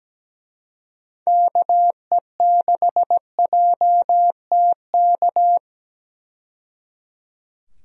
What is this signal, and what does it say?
Morse code, heard as a keyed tone, about 17 wpm, 700 Hz. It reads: KE6JTK
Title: USA Call Signs - 6 Characters - 17wpm